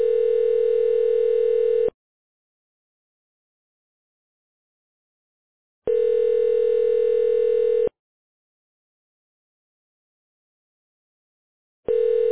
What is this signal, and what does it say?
Ringback tone (North America), 440 + 480 Hz.